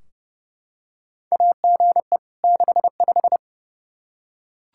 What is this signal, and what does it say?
Morse code, heard as a keyed tone, about 29 wpm, 700 Hz. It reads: AGE 65
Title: K1USN SST QSO Practice 30wpm